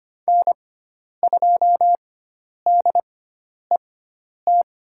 Morse code, heard as a keyed tone, about 25 wpm, spaced as for 10 wpm, 700 Hz. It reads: N2DET
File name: Lesson 16g - Letter c - Call Signs 25wpm